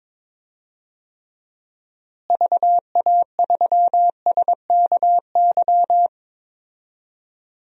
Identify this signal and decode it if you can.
Morse code, heard as a keyed tone, about 22 wpm, 700 Hz. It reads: VA3SKY
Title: DXCC-100 Call Signs - 6 Characters - Repeated 3x - 22wpm